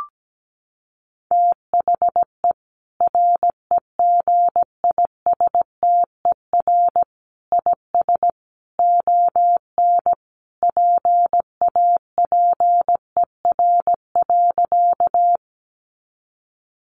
Morse code, heard as a keyed tone, about 17 wpm, 700 Hz. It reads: THE REGISTER IS ON PAPER.
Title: Sentences from Top 1200 Words 17wpm